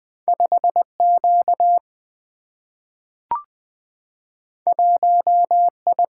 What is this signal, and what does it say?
Morse code, heard as a keyed tone, about 20 wpm, 700 Hz. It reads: 5Q 1I
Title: Single-Letter Single-Number 20wpm